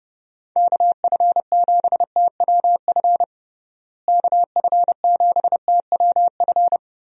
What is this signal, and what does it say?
Morse code, heard as a keyed tone, about 30 wpm, 700 Hz. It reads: KF7TWF KF7TWF
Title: USA Call Signs - 6 Characters - Repeated 3x - 30wpm